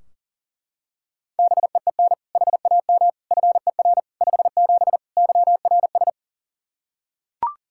Morse code, heard as a keyed tone, about 37 wpm, 700 Hz. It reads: BEEN HAM FER 57 YRS
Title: K1USN SST QSO Practice 40wpm